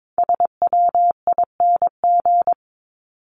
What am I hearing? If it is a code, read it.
Morse code, heard as a keyed tone, about 22 wpm, 700 Hz. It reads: SWING
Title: Top 1751-2000 Words 22wpm